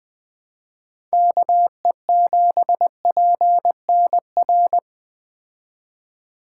Morse code, heard as a keyed tone, about 20 wpm, 700 Hz. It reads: KE7PNR
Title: Speed Racing - USA 2x3 Call Signs 17wpm